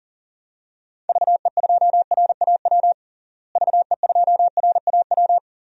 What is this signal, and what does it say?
Morse code, heard as a keyed tone, about 40 wpm, 700 Hz. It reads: VE2RAW VE2RAW
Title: DXCC-100 Call Signs - 6 Characters - Repeated 3x - 40wpm